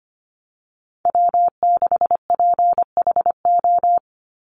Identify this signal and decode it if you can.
Morse code, heard as a keyed tone, about 25 wpm, 700 Hz. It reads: W6PHO
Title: Call Signs - 25wpm